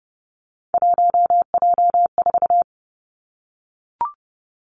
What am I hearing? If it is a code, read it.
Morse code, heard as a keyed tone, about 30 wpm, 700 Hz. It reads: 1J4
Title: Three Challenging Letter-Number Combinations 30wpm